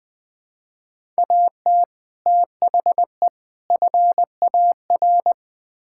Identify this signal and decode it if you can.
Morse code, heard as a keyed tone, about 20 wpm, 700 Hz. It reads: AT THE FAR
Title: Build A Phrase - 5-Word Phrases - 20wpm